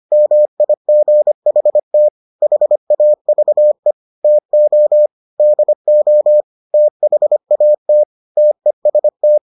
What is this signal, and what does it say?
Morse code, heard as a keyed tone, about 25 wpm, 600 Hz. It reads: MIGHT HAVE TO DO THAT TEST